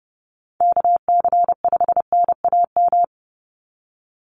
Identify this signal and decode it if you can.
Morse code, heard as a keyed tone, about 30 wpm, 700 Hz. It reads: KC5NAM
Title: USA Call Signs 30wpm